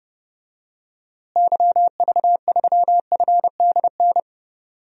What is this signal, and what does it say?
Morse code, heard as a keyed tone, about 30 wpm, 700 Hz. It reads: YV3FDN
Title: DXCC-100 Call Signs - 6 Characters - Repeated 3x - 30wpm